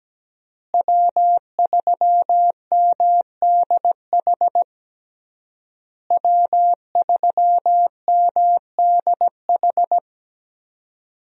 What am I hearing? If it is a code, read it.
Morse code, heard as a keyed tone, about 17 wpm, 700 Hz. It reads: W3MDH W3MDH
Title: USA Call Signs - 5 Characters - Repeated 3x - 17wpm